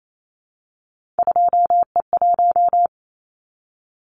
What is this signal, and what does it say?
Morse code, heard as a keyed tone, about 28 wpm, 700 Hz. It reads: 2E1
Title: Callsign Prefixes for Top 25 DXCC - Repeated 3x - 28wpm